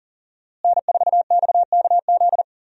Morse code, heard as a keyed tone, about 40 wpm, 700 Hz. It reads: N4XKZ
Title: USA Call Signs - 5 Characters - Repeated 2x - 40wpm